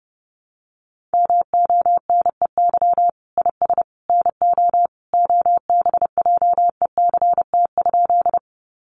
Morse code, heard as a keyed tone, about 30 wpm, 700 Hz. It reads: MONEY IS NO OBJECT?
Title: Sentences from Top 500 Words 30wpm